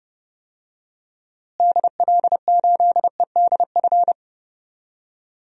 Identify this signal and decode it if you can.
Morse code, heard as a keyed tone, about 32 wpm, 700 Hz. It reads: DL8EDF
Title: DXCC-100 Call Signs - 6 Characters - Repeated 3x - 30wpm